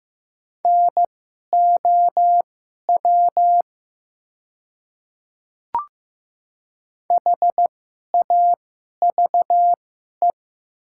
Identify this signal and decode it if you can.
Morse code, heard as a keyed tone, about 15 wpm, spaced as for 11 wpm, 700 Hz. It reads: NOW HAVE
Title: 2x Character Spacing - Top 100 Words - 15wpm